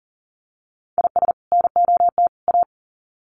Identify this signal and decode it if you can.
Morse code, heard as a keyed tone, about 40 wpm, 700 Hz. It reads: IS NOT A
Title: Sets of 3 Words 40wpm